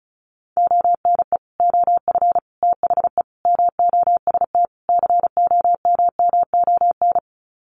Morse code, heard as a keyed tone, about 34 wpm, 700 Hz. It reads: ONE OF THE MOST COMMON
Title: Sets of 5 Words Encore2 35wpm